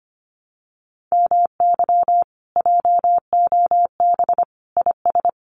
Morse code, heard as a keyed tone, about 25 wpm, 700 Hz. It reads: MY JOB IS
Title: Common QSO Phrases Encore 25wpm